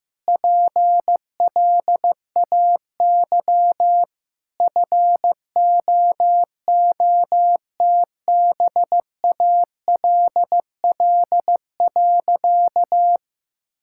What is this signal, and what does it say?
Morse code, heard as a keyed tone, about 15 wpm, 700 Hz. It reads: PLAY FOOTBALL.
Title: Sentences from Top 1600 Words 15wpm